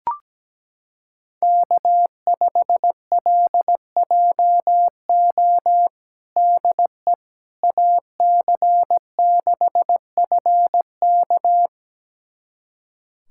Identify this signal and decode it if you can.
Morse code, heard as a keyed tone, about 17 wpm, 700 Hz. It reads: K5LJO DE AC6FK
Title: USA Callsign DE Callsign - 5 Characters - 17wpm